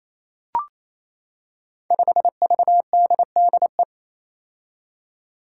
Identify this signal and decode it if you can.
Morse code, heard as a keyed tone, about 28 wpm, 700 Hz. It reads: 5VDDE
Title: Five Challenging Letter-Number Combinations 28wpm